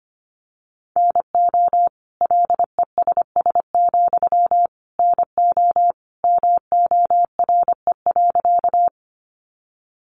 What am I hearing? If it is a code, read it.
Morse code, heard as a keyed tone, about 25 wpm, 700 Hz. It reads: NO LESS, NO MORE.
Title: Sentences from Top 400 Words 25wpm